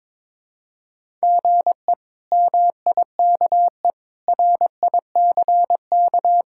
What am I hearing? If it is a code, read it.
Morse code, heard as a keyed tone, about 22 wpm, 700 Hz. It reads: GE MIKE RICK
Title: K1USN SST QSO Practice - 22wpm